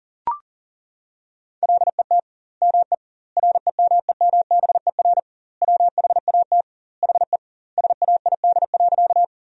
Morse code, heard as a keyed tone, about 40 wpm, 700 Hz. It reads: LET ME REMEMBER WHAT HE SAID.
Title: 2x Word-Spacing - Sentences from Top 400 Words - 40wpm